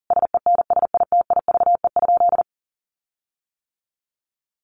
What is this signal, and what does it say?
Morse code, heard as a keyed tone, about 40 wpm, 700 Hz. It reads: SENSITIVE?
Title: Sentences from Top 1400 Words 40wpm